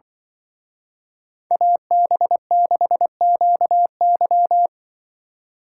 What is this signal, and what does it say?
Morse code, heard as a keyed tone, about 24 wpm, 700 Hz. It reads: AB6QY
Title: Speed Racing - USA Call Signs 20wpm